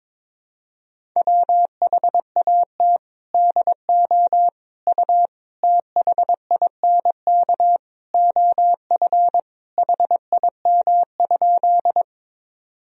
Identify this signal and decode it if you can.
Morse code, heard as a keyed tone, about 21 wpm, 700 Hz. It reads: WHAT DO U THINK OF HIM?
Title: Sentences from Top 200 Words and Abbreviations 22wpm